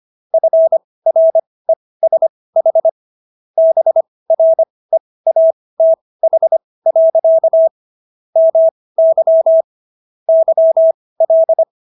Morse code, heard as a keyed tone, about 25 wpm, spaced as for 18 wpm, 650 Hz. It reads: FRESH BREATH. MY YL